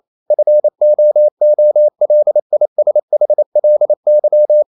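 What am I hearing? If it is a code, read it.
Morse code, heard as a keyed tone, about 28 wpm, 600 Hz. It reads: FOOLISHLY
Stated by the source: Morse code practice at 28 wpm, text of Ken Follett's novel Never in English, from CW med Gnister.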